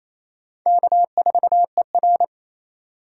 Morse code, heard as a keyed tone, about 28 wpm, 700 Hz. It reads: K4ER
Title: USA Call Signs - 4 Characters - 3x Character Spacing - 28wpm